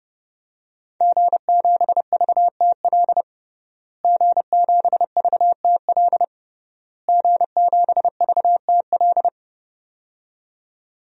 Morse code, heard as a keyed tone, about 31 wpm, 700 Hz. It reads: G7VTL G7VTL G7VTL
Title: DXCC-100 Call Signs - 5 Characters - Repeated 3x - 30wpm